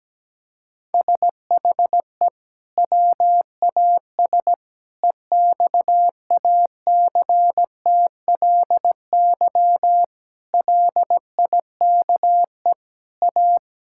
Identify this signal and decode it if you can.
Morse code, heard as a keyed tone, about 17 wpm, 700 Hz. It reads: SHE WAS EXACTLY LIKE A
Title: Sentences from Top 1000 Words - Review Part 6 - 17wpm